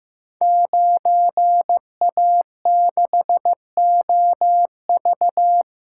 Morse code, heard as a keyed tone, about 15 wpm, 700 Hz. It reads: 9A6OV
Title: Code Groups - 5 Letter:Numbers - 15wpm